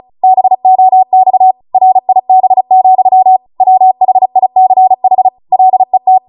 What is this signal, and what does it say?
Morse code, heard as a keyed tone, about 36 wpm, 750 Hz. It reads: BOX RIB, WHICH LET